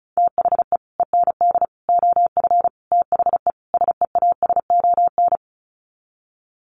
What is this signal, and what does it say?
Morse code, heard as a keyed tone, about 34 wpm, 700 Hz. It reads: THE END OF THE SEASON
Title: Sets of 5 Words Encore 35wpm